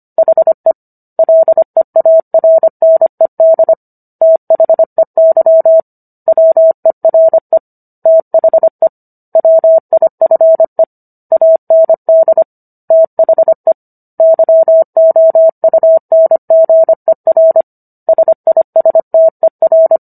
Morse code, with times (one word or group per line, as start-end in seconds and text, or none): 0.2-0.7 HE
1.2-3.7 LEARNED
4.2-5.8 THEY
6.3-7.6 WERE
8.1-8.9 THE
9.4-10.8 WIFE
11.3-12.4 AND
12.9-13.7 THE
14.2-17.6 YOUNGER
18.1-20.0 SISTER